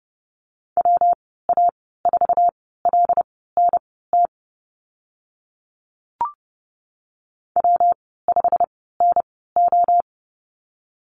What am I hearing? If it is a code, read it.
Morse code, heard as a keyed tone, about 30 wpm, spaced as for 17 wpm, 700 Hz. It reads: WA4LNT W5NO
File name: Lesson 19g - Letter w - Call Signs 30wpm